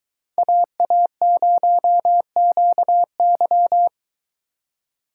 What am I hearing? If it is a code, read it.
Morse code, heard as a keyed tone, about 24 wpm, 700 Hz. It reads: AA0QY